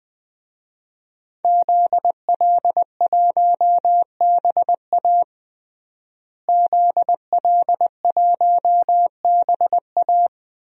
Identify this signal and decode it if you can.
Morse code, heard as a keyed tone, about 20 wpm, 700 Hz. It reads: ZL1BA ZL1BA
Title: DXCC-100 Call Signs - 5 Characters - Repeated 2x - 20wpm